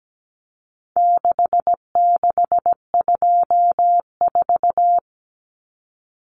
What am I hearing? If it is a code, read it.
Morse code, heard as a keyed tone, about 17 wpm, 700 Hz. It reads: 6624
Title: Numbers - 4-Digit 17wpm